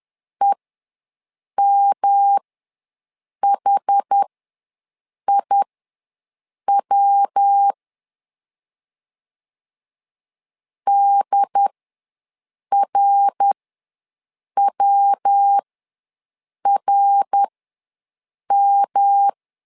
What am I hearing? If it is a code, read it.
Morse code, heard as a keyed tone, about 10 wpm, spaced as for 6 wpm, 800 Hz. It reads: EMHIW DRWRM
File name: LEZIONI / 3 MWRO / 3 MWR ws 10wpm 2151